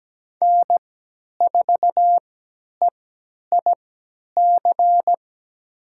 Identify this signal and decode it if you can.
Morse code, heard as a keyed tone, about 17 wpm, spaced as for 10 wpm, 700 Hz. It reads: N4EIC
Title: USA Call Signs - 5 Characters - 3x Character Spacing - 17wpm